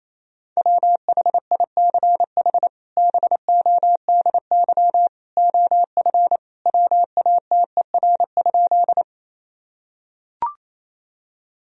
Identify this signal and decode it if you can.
Morse code, heard as a keyed tone, about 28 wpm, 700 Hz. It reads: WHICH BODY OF WATER?
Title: Sentences from Top 1000 Words - Review Part 2 - 28wpm